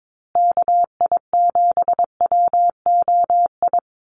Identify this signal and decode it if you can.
Morse code, heard as a keyed tone, about 22 wpm, 700 Hz. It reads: KI7WOI
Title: USA Call Signs Repeated 2x 22wpm